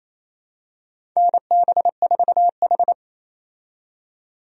Morse code, heard as a keyed tone, about 28 wpm, 700 Hz. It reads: NB4H
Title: USA Call Signs - 4 Characters - 28wpm